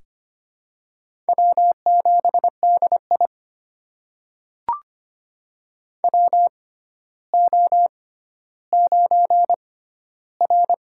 Morse code, heard as a keyed tone, about 25 wpm, spaced as for 13 wpm, 700 Hz. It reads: W7DI WO9R
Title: USA Call Signs - 4 Characters - 6x Character Spacing - 25wpm